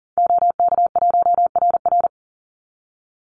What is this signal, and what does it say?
Morse code, heard as a keyed tone, about 40 wpm, 700 Hz. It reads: OK1RR
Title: Call Signs - 40wpm